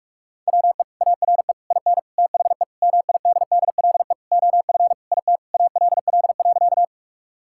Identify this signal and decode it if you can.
Morse code, heard as a keyed tone, about 41 wpm, 700 Hz. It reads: WE ARE IN THE MIDDLE OF IT ALL.
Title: Sentences from Top 1000 Words - Review Part 6 - 45wpm